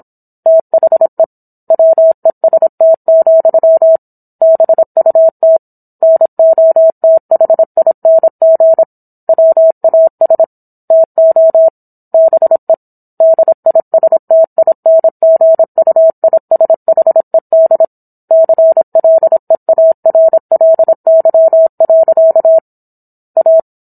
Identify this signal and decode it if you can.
Morse code, heard as a keyed tone, about 27 wpm, 650 Hz. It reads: THE WEST, BUT NOTHING WAS TO BE DISTINGUISHED CLEARLY. A
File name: Book One / 17-Chapter 17 - 26 WPM